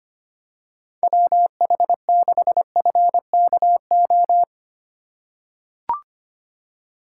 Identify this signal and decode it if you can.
Morse code, heard as a keyed tone, about 25 wpm, 700 Hz. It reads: WH6FKO
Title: USA Call Signs - 6 Characters - 2x Character Spacing - 25wpm